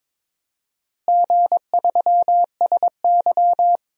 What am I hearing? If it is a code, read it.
Morse code, heard as a keyed tone, about 22 wpm, 700 Hz. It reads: G3SY
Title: DXCC-100 Call Signs - 4 Characters - Repeated 3x - 22wpm